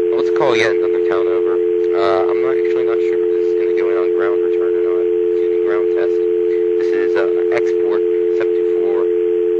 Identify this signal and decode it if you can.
Dial tone (North America), 350 + 440 Hz, with speech.
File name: TurtleCreekEWSD